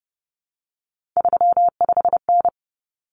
Morse code, heard as a keyed tone, about 29 wpm, 700 Hz. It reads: 35N
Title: Common Cut Numbers - 3-Digit 30wpm